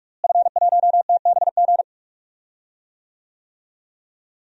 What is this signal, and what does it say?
Morse code, heard as a keyed tone, about 42 wpm, 700 Hz. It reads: F1TBG
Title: DXCC-100 Call Signs - 5 Characters - Repeated 3x - 45wpm